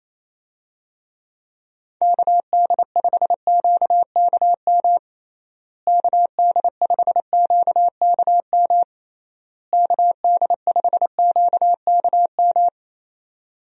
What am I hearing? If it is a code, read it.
Morse code, heard as a keyed tone, about 28 wpm, 700 Hz. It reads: KD5QKM KD5QKM KD5QKM
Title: USA Call Signs - 6 Characters - Repeated 3x - 28wpm